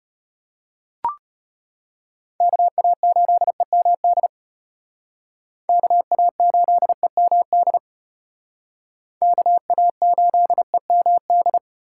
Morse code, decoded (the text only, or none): KA8EMD KA8EMD KA8EMD